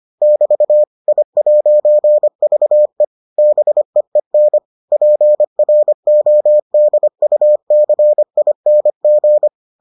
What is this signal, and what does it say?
Morse code, heard as a keyed tone, about 25 wpm, 600 Hz. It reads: <BT> I'VE BEEN PRODUCING